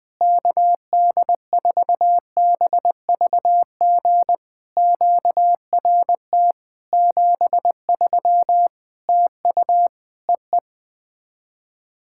Morse code, heard as a keyed tone, about 20 wpm, 700 Hz. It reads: KD4BVG QRT 73 TU EE
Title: POTA QSO Practice - 20wpm